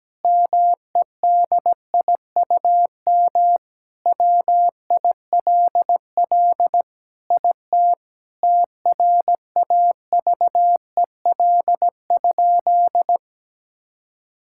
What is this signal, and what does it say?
Morse code, heard as a keyed tone, about 17 wpm, 700 Hz. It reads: MEDIUM WILL IT TRAVEL?
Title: Sentences from Top 1000 Words - Review Part 6 - 17wpm